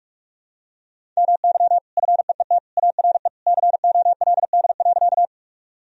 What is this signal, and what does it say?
Morse code, heard as a keyed tone, about 46 wpm, 700 Hz. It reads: MY FEET ARE COLD.
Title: Sentences from Top 1000 Words - Review Part 1 - 45wpm